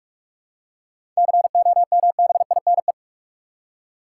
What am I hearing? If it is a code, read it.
Morse code, heard as a keyed tone, about 46 wpm, 700 Hz. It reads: COMBINE
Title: Seven Letter Words - 45wpm